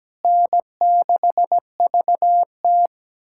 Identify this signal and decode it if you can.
Morse code, heard as a keyed tone, about 17 wpm, 700 Hz. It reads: N6VT
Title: USA Callsign DE Callsign - 4 Characters - 17wpm